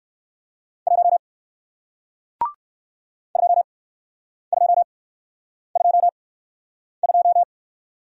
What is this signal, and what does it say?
Morse code, heard as a keyed tone, about 57 wpm, spaced as for 9 wpm, 700 Hz. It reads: 1 2222